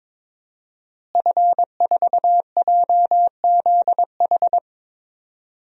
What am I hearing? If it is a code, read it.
Morse code, heard as a keyed tone, about 22 wpm, 700 Hz. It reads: F4JZH
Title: DXCC-100 Call Signs - 5 Characters - Repeated 2x - 22wpm